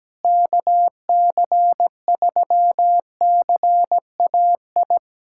Morse code, heard as a keyed tone, about 17 wpm, 700 Hz. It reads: KC3CAI
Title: USA Callsign DE Callsign - 6 Characters - 17wpm